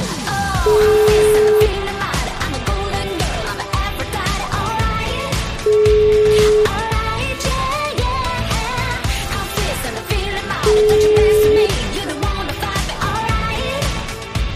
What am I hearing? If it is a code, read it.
Ringback tone (Europe, 425 Hz), with noise.